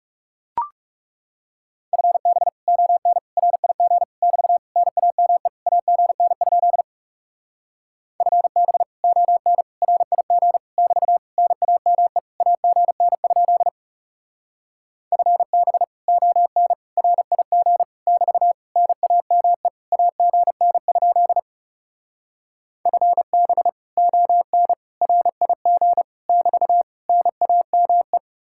Morse code, heard as a keyed tone, about 40 wpm, 700 Hz. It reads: FB ON RIG <BT> NAME AGN? FB ON RIG <BT> NAME AGN? FB ON RIG <BT> NAME AGN? FB ON RIG <BT> NAME